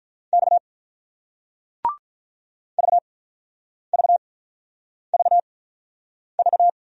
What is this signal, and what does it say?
Morse code, heard as a keyed tone, about 49 wpm, spaced as for 9 wpm, 700 Hz. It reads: X VVVV